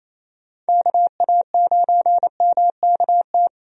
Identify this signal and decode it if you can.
Morse code, heard as a keyed tone, about 28 wpm, 700 Hz. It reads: KA9MKT
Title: USA Call Signs - 6 Characters - Repeated 2x - 28wpm